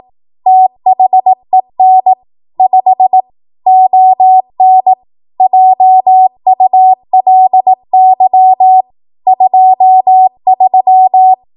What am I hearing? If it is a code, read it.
Morse code, heard as a keyed tone, about 18 wpm, 750 Hz. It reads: THEN 5 ON JULY 23
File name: Archive / 18 WPM / 18WPM